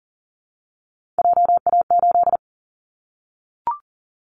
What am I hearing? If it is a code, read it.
Morse code, heard as a keyed tone, about 40 wpm, 700 Hz. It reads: JA8
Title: Callsign Prefixes for Top 50 DXCC - 40wpm